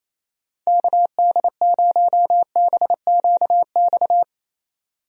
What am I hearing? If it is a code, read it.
Morse code, heard as a keyed tone, about 28 wpm, 700 Hz. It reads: KD0BQX
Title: USA Call Signs - 6 Characters - 28wpm